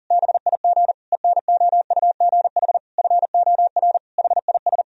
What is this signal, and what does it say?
Morse code, heard as a keyed tone, about 40 wpm, 700 Hz. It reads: BIG ENOUGH FOR HIS
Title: Sentences from Top 1000 Words - Review Part 2 - 40wpm